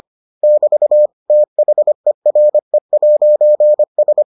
Morse code, heard as a keyed tone, about 25 wpm, 600 Hz. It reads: <BT> THERE'S